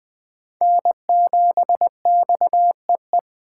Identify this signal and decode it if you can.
Morse code, heard as a keyed tone, about 21 wpm, 700 Hz. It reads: N7XEE
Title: Speed Racing - USA 1x3 Call Signs 17wpm